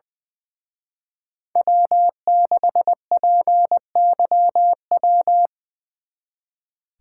Morse code, heard as a keyed tone, about 23 wpm, 700 Hz. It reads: W6PYW
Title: Speed Racing - USA 1x3 Call Signs 20wpm